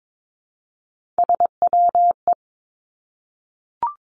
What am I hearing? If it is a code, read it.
Morse code, heard as a keyed tone, about 22 wpm, 700 Hz. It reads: SWE
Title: Three Challenging Letter-Number Combinations 22wpm